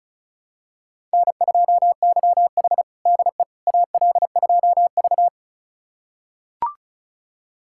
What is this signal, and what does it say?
Morse code, heard as a keyed tone, about 36 wpm, 700 Hz. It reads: N2YH DE AL2V
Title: USA Callsign DE Callsign - 4 Characters - 35wpm